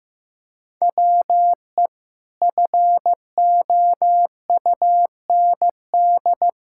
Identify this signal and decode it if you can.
Morse code, heard as a keyed tone, about 15 wpm, 700 Hz. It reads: WE FOUND
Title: Sentences from Top 1000 Words - Review Part 5 - 15wpm